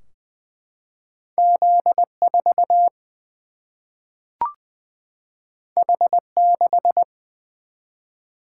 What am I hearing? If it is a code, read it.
Morse code, heard as a keyed tone, about 20 wpm, 700 Hz. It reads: Z 4 H 6
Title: Single-Letter Single-Number 20wpm